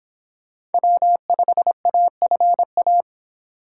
Morse code, heard as a keyed tone, about 29 wpm, 700 Hz. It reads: W5AFA